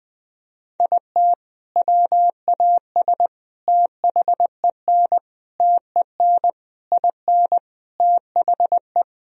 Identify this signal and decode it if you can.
Morse code, heard as a keyed tone, about 21 wpm, 700 Hz. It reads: IT WAS THEN TEN IN THE
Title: Sentences from Top 1000 Words - Review Part 2 - 20wpm